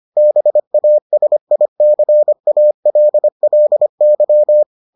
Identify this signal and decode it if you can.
Morse code, heard as a keyed tone, about 25 wpm, 600 Hz. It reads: BASICALLY